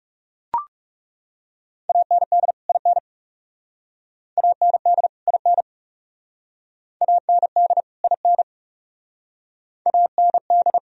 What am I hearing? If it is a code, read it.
Morse code, heard as a keyed tone, about 40 wpm, 700 Hz. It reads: AND IN AND IN AND IN AND